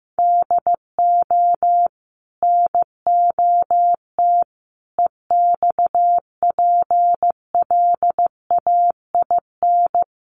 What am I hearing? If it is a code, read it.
Morse code, heard as a keyed tone, about 15 wpm, 700 Hz. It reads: DO NOT EXPLAIN
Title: Sentences from Top 1100 Words 15wpm